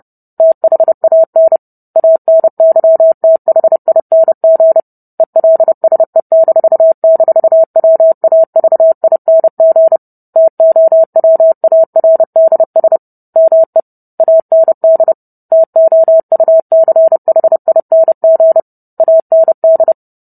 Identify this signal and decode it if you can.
Morse code, heard as a keyed tone, about 30 wpm, 650 Hz. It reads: THAN ANYTHING ELSE--WAVING TOWARDS ME AND TOUCHING AND